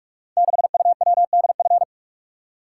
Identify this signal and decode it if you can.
Morse code, heard as a keyed tone, about 46 wpm, 700 Hz. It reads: 6UWDF
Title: Code Groups - 5 Letter:Numbers - 45wpm